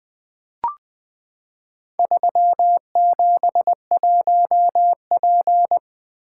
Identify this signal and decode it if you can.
Morse code, heard as a keyed tone, about 20 wpm, 700 Hz. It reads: 371P
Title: Four Letter-Number Combinations - 20wpm